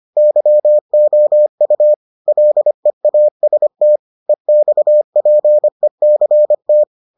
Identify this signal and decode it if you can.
Morse code, heard as a keyed tone, about 25 wpm, 600 Hz. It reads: YOU LEAST EXPECT